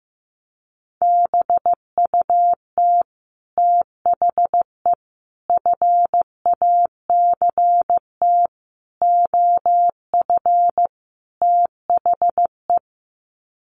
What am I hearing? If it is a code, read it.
Morse code, heard as a keyed tone, about 15 wpm, 700 Hz. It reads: BUT THE FACT OF THE
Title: Sets of 5 Words Encore2 15wpm